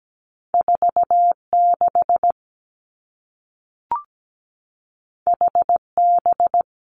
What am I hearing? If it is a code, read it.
Morse code, heard as a keyed tone, about 17 wpm, 700 Hz. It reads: 4 6 H B
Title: Two Challenging Letter-Number Combinations 17wpm